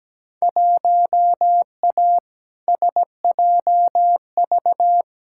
Morse code, heard as a keyed tone, about 17 wpm, 700 Hz. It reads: 1A SJV
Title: Field Day Exchanges Encore - 17wpm